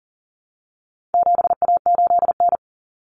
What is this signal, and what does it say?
Morse code, heard as a keyed tone, about 40 wpm, 700 Hz. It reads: 7A8N
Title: Common Cut Numbers - 4-Digit 40wpm